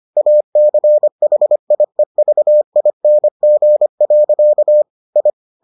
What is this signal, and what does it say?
Morse code, heard as a keyed tone, about 25 wpm, 600 Hz. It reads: ACHIEVING. I